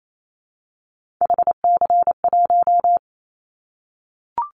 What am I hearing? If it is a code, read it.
Morse code, heard as a keyed tone, about 28 wpm, 700 Hz. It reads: HC1
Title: Callsign Prefixes for Top 100 DXCC - 28wpm